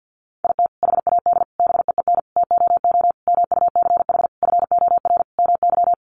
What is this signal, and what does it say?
Morse code, heard as a keyed tone, about 50 wpm, 700 Hz. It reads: IT HAD BEEN TOO MUCH FOR MY
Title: Sentences from Top 1300 Words 50wpm